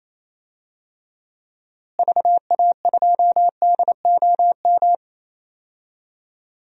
Morse code, heard as a keyed tone, about 27 wpm, 700 Hz. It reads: VA2DOM
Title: DXCC-100 Call Signs - 6 Characters - Repeated 3x - 28wpm